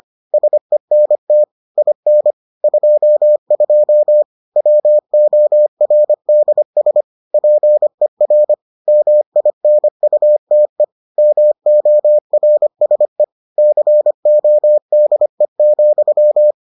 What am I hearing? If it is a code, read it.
Morse code, heard as a keyed tone, about 25 wpm, 600 Hz. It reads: SENT IN 22 WORDS PER MINUTE MORSE CODE,